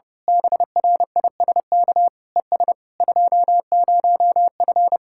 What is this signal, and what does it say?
Morse code, heard as a keyed tone, about 29 wpm, 700 Hz. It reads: BRISK ES 20F
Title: Speed-Racing Common QSO Phrases - 30wpm